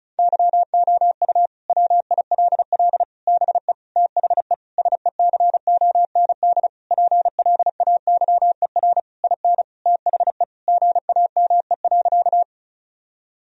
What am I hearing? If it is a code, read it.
Morse code, heard as a keyed tone, about 36 wpm, 700 Hz. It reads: YOU WILL BE THE SECOND PLAYER IN THE GAME.
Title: Sentences from Top 1000 Words - Review Part 4 - 35wpm